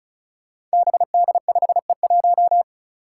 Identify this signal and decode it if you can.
Morse code, heard as a keyed tone, about 36 wpm, 700 Hz. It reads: BD5E1
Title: Five Challenging Letter-Number Combinations 35wpm